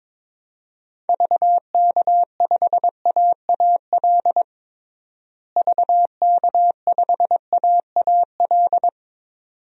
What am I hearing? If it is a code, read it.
Morse code, heard as a keyed tone, about 22 wpm, 700 Hz. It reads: VK5AAL VK5AAL
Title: DXCC-100 Call Signs - 6 Characters - Repeated 2x - 22wpm